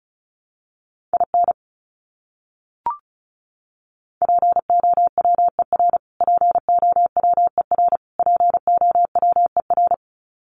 Morse code, heard as a keyed tone, about 34 wpm, 700 Hz. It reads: IN POWER POWER POWER
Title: Top 500 Words Repeated 35wpm